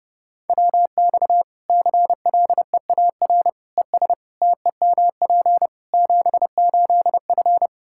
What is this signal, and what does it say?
Morse code, heard as a keyed tone, about 30 wpm, 700 Hz. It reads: WX CLEAR ES TEMP 78F
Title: K1USN SST QSO Practice 30wpm